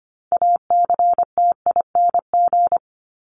Morse code, heard as a keyed tone, about 25 wpm, 700 Hz. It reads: ACTING
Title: ING Suffix - Top 100 Words with Base Words 25wpm